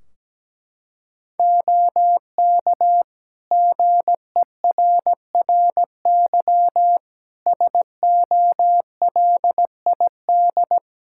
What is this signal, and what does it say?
Morse code, heard as a keyed tone, about 17 wpm, 700 Hz. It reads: OK GERRY SOLID
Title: Practice QSO - Exchange 3 - More Info - 17wpm